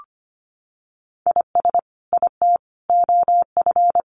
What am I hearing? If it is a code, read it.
Morse code, heard as a keyed tone, about 25 wpm, 700 Hz. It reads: IS IT OF
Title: Sentences from Top 1500 Words 25wpm